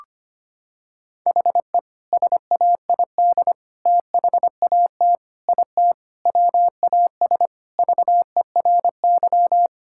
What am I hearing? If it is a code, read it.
Morse code, heard as a keyed tone, about 25 wpm, 700 Hz. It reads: HE SAID THAT IT WAS VERY
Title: Sentences from Top 1000 Words - Review Part 6 - 25wpm